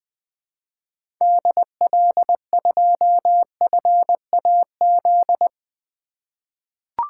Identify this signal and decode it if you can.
Morse code, heard as a keyed tone, about 20 wpm, 700 Hz. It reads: DL2FAZ
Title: 2023 SOTA Activator Call Signs - 20wpm